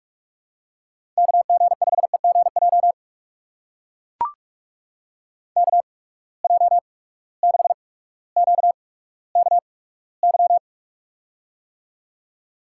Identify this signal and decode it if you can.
Morse code, heard as a keyed tone, about 44 wpm, spaced as for 19 wpm, 700 Hz. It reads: KG5EGJ KJ6QKY
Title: USA Call Signs - 6 Characters - 8x Character Spacing - 45wpm